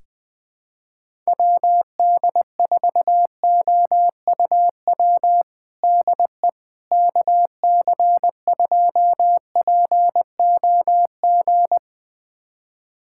Morse code, heard as a keyed tone, about 20 wpm, 700 Hz. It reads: WD4OUW DE KC2POG
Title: USA Callsign DE Callsign 20wpm